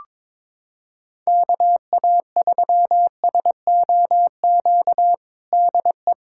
Morse code, heard as a keyed tone, about 22 wpm, 700 Hz. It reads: KA3SOQ DE
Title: USA Callsign DE Callsign 22wpm